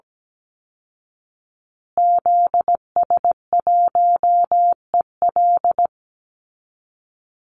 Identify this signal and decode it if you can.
Morse code, heard as a keyed tone, about 17 wpm, 700 Hz. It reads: ZS1EL
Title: Call Signs Repeated 2x - 17wpm